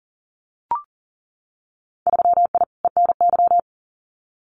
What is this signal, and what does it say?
Morse code, heard as a keyed tone, about 40 wpm, 700 Hz. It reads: 3I ENY
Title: Winter Field Day Exchanges Encore 40wpm